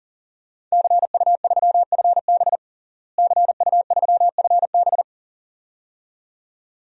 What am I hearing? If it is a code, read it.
Morse code, heard as a keyed tone, about 37 wpm, 700 Hz. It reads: CU3FB CU3FB
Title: DXCC-100 Call Signs - 5 Characters - Repeated 2x - 40wpm